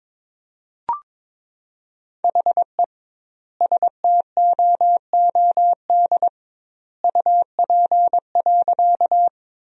Morse code, heard as a keyed tone, about 22 wpm, 700 Hz. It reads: HE STOOD UP.
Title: 2x Word-Spacing - Sentences from Top 500 Words - 22wpm